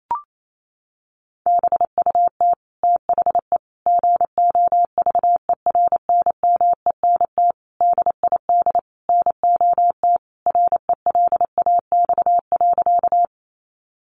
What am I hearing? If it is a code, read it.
Morse code, heard as a keyed tone, about 28 wpm, 700 Hz. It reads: BUT THE GOVERNMENT DID NOT RELAX.
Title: Sentences from Top 1400 Words 28wpm